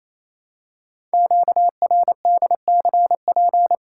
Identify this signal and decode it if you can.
Morse code, heard as a keyed tone, about 28 wpm, 700 Hz. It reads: QRDCP
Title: Code Groups - 5 Letter:Numbers - 28wpm